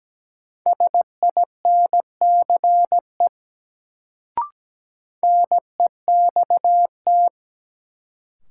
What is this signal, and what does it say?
Morse code, heard as a keyed tone, about 17 wpm, 700 Hz. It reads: SINCE NEXT
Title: Sentences from Top 300 Words - New Word Drill - 17wpm